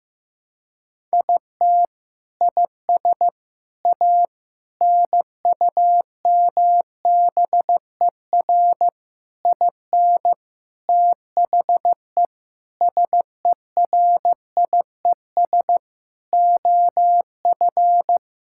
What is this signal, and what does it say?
Morse code, heard as a keyed tone, about 15 wpm, 700 Hz. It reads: IT IS A NUMBER IN THE SERIES OF
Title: Sentences from Top 1000 Words - Review Part 5 - 15wpm